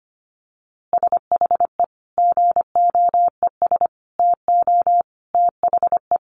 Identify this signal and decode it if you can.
Morse code, heard as a keyed tone, about 25 wpm, 700 Hz. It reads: SHE GOES TO THE
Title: Easy Phrases 25wpm